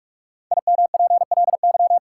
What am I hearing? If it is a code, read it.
Morse code, heard as a keyed tone, about 46 wpm, 700 Hz. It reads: IMPLY
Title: Five Letter Words - 45wpm